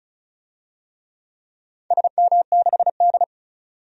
Morse code, heard as a keyed tone, about 34 wpm, 700 Hz. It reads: SM6D
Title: DXCC-100 Call Signs - 4 Characters - 35wpm